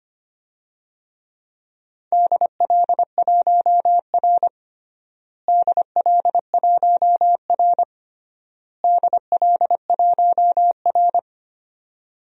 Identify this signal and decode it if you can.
Morse code, heard as a keyed tone, about 25 wpm, 700 Hz. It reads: DL1R DL1R DL1R
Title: DXCC-100 Call Signs - 4 Characters - Repeated 3x - 25wpm